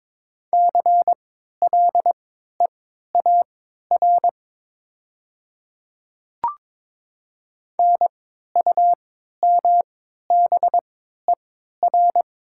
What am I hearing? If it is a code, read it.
Morse code, heard as a keyed tone, about 22 wpm, spaced as for 13 wpm, 700 Hz. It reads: CLEAR NUMBER
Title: 3x Character Spacing - Top 500 Words - 22wpm